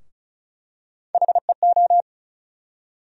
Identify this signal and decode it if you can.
Morse code, heard as a keyed tone, about 36 wpm, 700 Hz. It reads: HEO
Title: USA Call Sign Postfixes - 3 Letters 35wpm